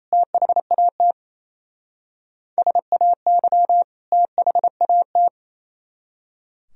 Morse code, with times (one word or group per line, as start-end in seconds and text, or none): 0.1-1.1 THAT
2.6-3.8 SAY
4.1-5.3 THAT